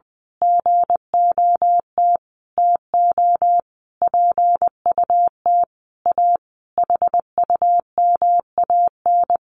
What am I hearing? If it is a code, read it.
Morse code, heard as a keyed tone, about 20 wpm, 700 Hz. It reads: GOT TO PUT A HUMAN
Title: Sentences from Top 1200 Words 20wpm